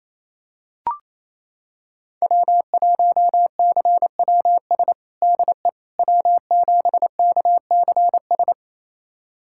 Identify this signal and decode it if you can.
Morse code, heard as a keyed tone, about 28 wpm, 700 Hz. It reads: W1CWS DE W7KCS
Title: USA Callsign DE Callsign - 5 Characters - 28wpm